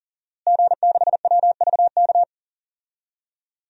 Morse code, heard as a keyed tone, about 40 wpm, 700 Hz. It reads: G6WVK
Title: DXCC-100 Call Signs - 5 Characters - Repeated 3x - 40wpm